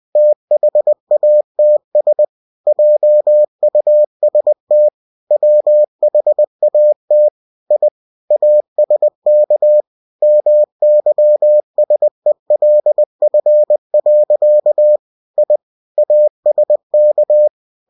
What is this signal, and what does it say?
Morse code, heard as a keyed tone, about 20 wpm, 600 Hz. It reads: THATS JUST WHAT I ASK MYSELF. I ASK
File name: winnie-20wpm